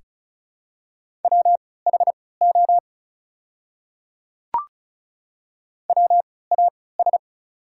Morse code, heard as a keyed tone, about 36 wpm, spaced as for 20 wpm, 690 Hz. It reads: WHO WAS
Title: 3x Character Spacing - Top 100 Words - 35wpm